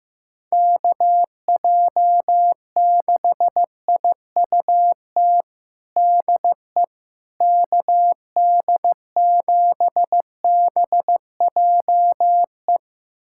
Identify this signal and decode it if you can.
Morse code, heard as a keyed tone, about 16 wpm, 700 Hz. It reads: KJ6IUT DE KD7BJE
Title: USA Callsign DE Callsign 15wpm